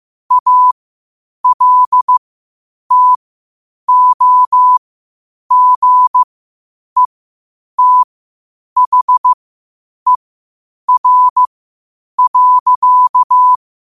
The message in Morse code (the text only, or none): ALTOGETHER.